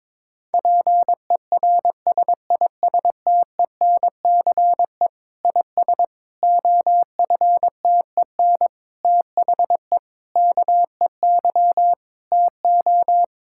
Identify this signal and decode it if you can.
Morse code, heard as a keyed tone, about 22 wpm, 700 Hz. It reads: PERSISTENCE IS OFTEN THE KEY TO
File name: Difficult Sentences with Hints - 22wpm